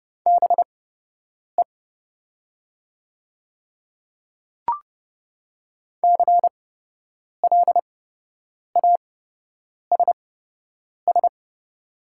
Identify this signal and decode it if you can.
Morse code, heard as a keyed tone, about 30 wpm, spaced as for 8 wpm, 700 Hz. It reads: BE CLASS
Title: 8x Character Spacing - Top 500 Words - 30wpm